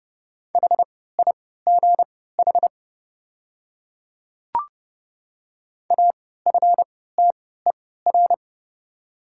Morse code, heard as a keyed tone, about 31 wpm, spaced as for 17 wpm, 700 Hz. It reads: HIGH AFTER
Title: 3x Character Spacing - Top 200 Words - 30wpm